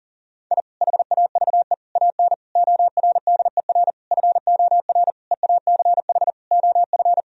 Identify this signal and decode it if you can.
Morse code, heard as a keyed tone, about 40 wpm, 700 Hz. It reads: I HAVE AN ORDER FOR EACH OF
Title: Sentences from Top 1000 Words - Review Part 2 - 40wpm